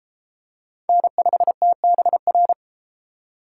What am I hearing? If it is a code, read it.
Morse code, heard as a keyed tone, about 31 wpm, 700 Hz. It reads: N5TBR